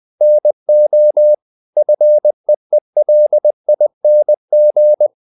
Morse code, heard as a keyed tone, about 20 wpm, 600 Hz. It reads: NO FEELING
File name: the old man and the sea part 2 20wpm